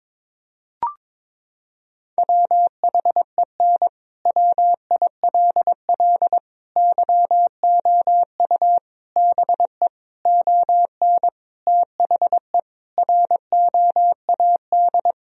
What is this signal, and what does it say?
Morse code, heard as a keyed tone, about 22 wpm, 700 Hz. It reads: WHEN WILL YOU BE ON THE ROAD
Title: Sentences from Top 1000 Words - Review Part 2 - 22wpm